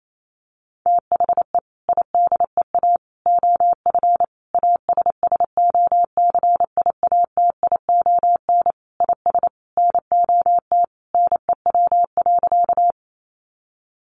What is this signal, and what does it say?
Morse code, heard as a keyed tone, about 28 wpm, 700 Hz. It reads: THE IDEA OF ASSOCIATION IS NOT NEW.
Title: Sentences from Top 1100 Words 28wpm